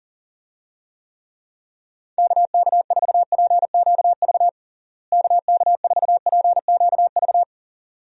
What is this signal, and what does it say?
Morse code, heard as a keyed tone, about 40 wpm, 700 Hz. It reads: KK4PQV KK4PQV
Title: USA Call Signs - 6 Characters - Repeated 2x - 40wpm